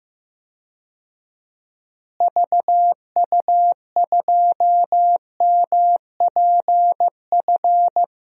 Morse code, heard as a keyed tone, about 15 wpm, 700 Hz. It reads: VU2MPF
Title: DXCC-100 Call Signs - 6 Characters - Repeated 2x - 15wpm